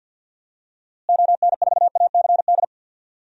Morse code, heard as a keyed tone, about 50 wpm, 700 Hz. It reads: ON4ACD
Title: DXCC-100 Call Signs - 6 Characters - Repeated 2x - 50wpm